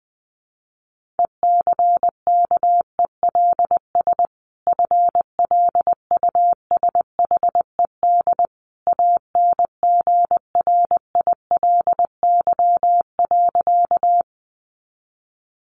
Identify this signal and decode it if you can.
Morse code, heard as a keyed tone, about 20 wpm, 700 Hz. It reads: ECKELS FLUSHED ANGRILY.
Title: A Sound Of Thunder 20wpm